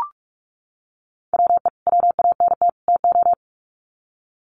Morse code, heard as a keyed tone, about 44 wpm, 700 Hz. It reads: WE WANT TO
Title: Sets of 3 Words 45wpm